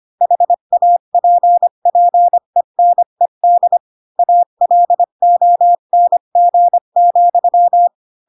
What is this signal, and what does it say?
Morse code, heard as a keyed tone, about 25 wpm, 700 Hz. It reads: HAPPENED ALONG,
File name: Pinocchio / Pinocchio ch3 20 25 ik1hge